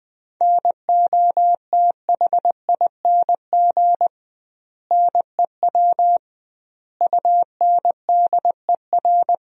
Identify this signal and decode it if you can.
Morse code, heard as a keyed tone, about 20 wpm, 700 Hz. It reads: NOTHING NEW UNDER
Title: 2x Word-Spacing - Sentences from Top 500 Words - 20wpm